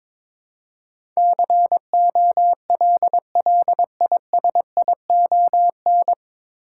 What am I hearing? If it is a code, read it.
Morse code, heard as a keyed tone, about 22 wpm, 700 Hz. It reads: COLLISION
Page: Morse Code Ninja practice 22wpm